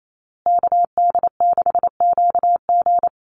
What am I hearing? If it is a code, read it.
Morse code, heard as a keyed tone, about 28 wpm, 700 Hz. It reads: KD6QG
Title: USA Call Signs Repeated 3x 28wpm